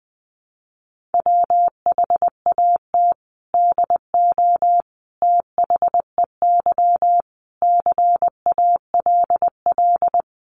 Morse code, heard as a keyed tone, about 20 wpm, 700 Hz. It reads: WHAT DO THEY CALL
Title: Sentences from Top 100 Words 20wpm